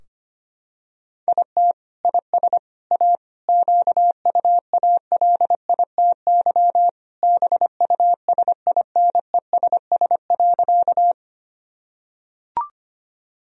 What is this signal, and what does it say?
Morse code, heard as a keyed tone, about 25 wpm, 690 Hz. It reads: IT IS A QUALITY BUSINESS.
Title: Sentences from Top 1000 Words - Review Part 4 - 25wpm